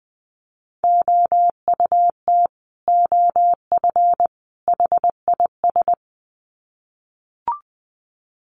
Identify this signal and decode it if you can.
Morse code, heard as a keyed tone, about 20 wpm, 700 Hz. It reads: OUT OF HIS
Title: Sets of 3 Words 20wpm